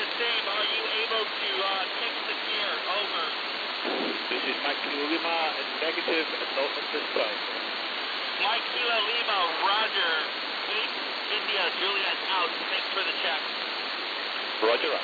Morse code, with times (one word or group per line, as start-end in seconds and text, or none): none